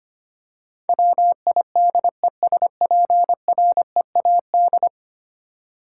Morse code, heard as a keyed tone, about 25 wpm, 700 Hz. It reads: WIDESPREAD
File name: 10 Letter Words - First 3 Letters Given - 25wpm